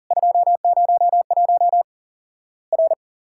Morse code, heard as a keyed tone, about 40 wpm, 700 Hz. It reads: 201 R
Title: Worked All Europe DX Contest - QTC Exchange - 40wpm